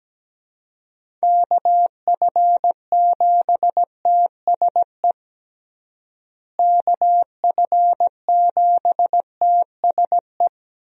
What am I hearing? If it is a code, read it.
Morse code, heard as a keyed tone, about 17 wpm, 700 Hz. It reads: KF7TSE KF7TSE
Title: USA Call Signs - 6 Characters - Repeated 3x - 17wpm